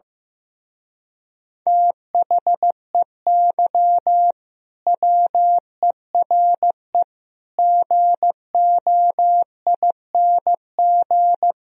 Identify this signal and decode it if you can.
Morse code, heard as a keyed tone, about 15 wpm, 700 Hz. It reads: THEY WERE GOING
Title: Speed-Racing - Sets of 4 Words 15wpm